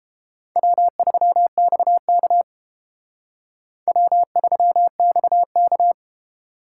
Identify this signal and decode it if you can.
Morse code, heard as a keyed tone, about 30 wpm, 700 Hz. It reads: W3XK W3XK